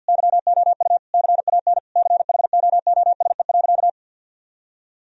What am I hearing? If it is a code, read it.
Morse code, heard as a keyed tone, about 50 wpm, 700 Hz. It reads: YOU CAN CHOOSE.
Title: Sentences from Top 1000 Words - Review Part 4 - 50wpm